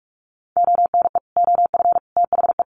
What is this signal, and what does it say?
Morse code, heard as a keyed tone, about 44 wpm, 700 Hz. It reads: ONE OF THE
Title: Sets of 3 Words 45wpm